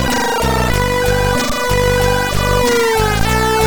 There is no code.